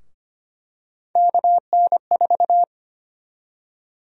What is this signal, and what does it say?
Morse code, heard as a keyed tone, about 25 wpm, 700 Hz. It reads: KN4
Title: USA Call Sign Prefixes - Repeated 2x - 25wpm